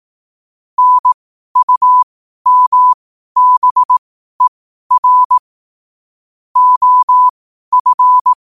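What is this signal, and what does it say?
Morse code, heard as a keyed tone, about 18 wpm, spaced as for 12 wpm, 1000 Hz. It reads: NUMBER OF
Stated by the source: 12 wpm Morse.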